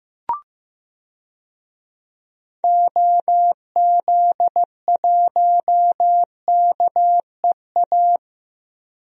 Morse code, heard as a keyed tone, about 16 wpm, 700 Hz. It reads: OZ1KEA
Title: DXCC-100 Call Signs - 6 Characters - Repeated 2x - 15wpm